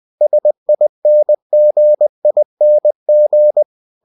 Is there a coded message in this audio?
Yes: Morse code from 0.2 to 3.6 s, heard as a keyed tone, about 20 wpm, 600 Hz. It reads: SINGING